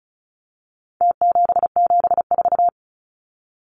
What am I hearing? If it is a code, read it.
Morse code, heard as a keyed tone, about 34 wpm, 700 Hz. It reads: T774
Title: Numbers - Contest Style - 4-Digit 35wpm